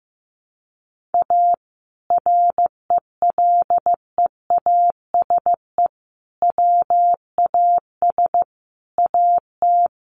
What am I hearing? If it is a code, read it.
Morse code, heard as a keyed tone, about 14 wpm, 700 Hz. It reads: A RELEASE WAS AT
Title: Sentences from Top 1100 Words 15wpm